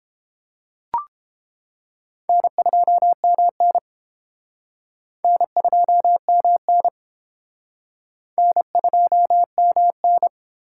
Morse code, heard as a keyed tone, about 31 wpm, 700 Hz. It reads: N2MN N2MN N2MN